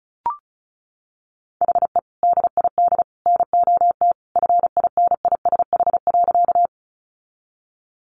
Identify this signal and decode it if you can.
Morse code, heard as a keyed tone, about 34 wpm, 700 Hz. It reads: HE DID NOT FINISH.
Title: Sentences from Top 1200 Words 35wpm